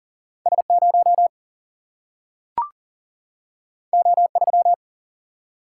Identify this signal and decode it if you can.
Morse code, heard as a keyed tone, about 40 wpm, 700 Hz. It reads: S0 O3
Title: Single-Letter Single-Number 40wpm